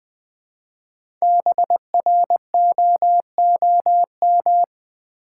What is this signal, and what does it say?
Morse code, heard as a keyed tone, about 20 wpm, 700 Hz. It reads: BROOM
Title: Easy Words with Hints - 20wpm